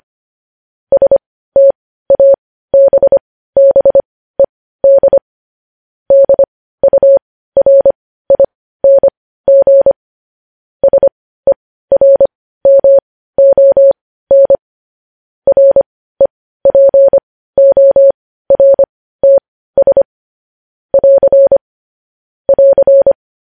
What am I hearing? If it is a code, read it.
Morse code, heard as a keyed tone, about 25 wpm, spaced as for 15 wpm, 550 Hz. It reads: STABBED DURING SERMON REPORTS <AR> <AR>